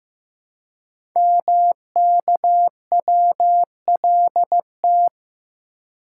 Morse code, heard as a keyed tone, about 15 wpm, 700 Hz. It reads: MKWLT
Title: Code Groups - 5 Letter:Numbers - 15wpm